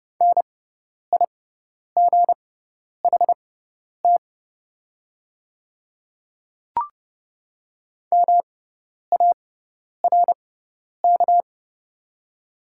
Morse code, heard as a keyed tone, about 30 wpm, spaced as for 10 wpm, 700 Hz. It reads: NIGHT MARK
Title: 6x Character Spacing - Top 300 Words - 30wpm